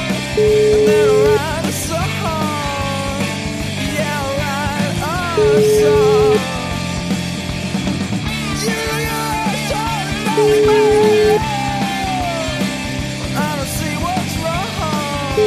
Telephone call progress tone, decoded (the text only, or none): ringback tone